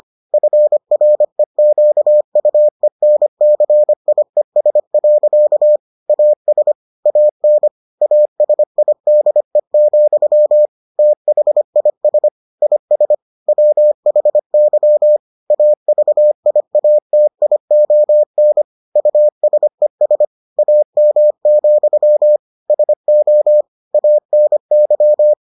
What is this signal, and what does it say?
Morse code, heard as a keyed tone, about 25 wpm, 600 Hz. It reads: FREQUENCIES. AS AN ASIDE, THIS IS WHY AVIATION USES AM, SO ANY